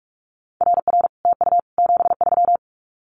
Morse code, heard as a keyed tone, about 46 wpm, 700 Hz. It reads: RR TU 73
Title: Common QSO Phrases Encore 45wpm